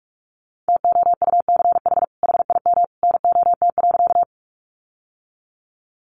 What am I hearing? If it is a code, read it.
Morse code, heard as a keyed tone, about 44 wpm, 700 Hz. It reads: TOUCH HIM NOT.
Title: Sentences from Top 1200 Words 45wpm